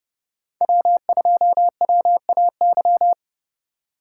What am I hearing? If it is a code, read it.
Morse code, heard as a keyed tone, about 30 wpm, 700 Hz. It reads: W2WAY